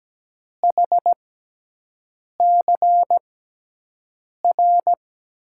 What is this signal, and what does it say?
Morse code, heard as a keyed tone, about 17 wpm, spaced as for 6 wpm, 700 Hz. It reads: HCR
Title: USA Call Signs - 5 Characters - 6x Character Spacing - 17wpm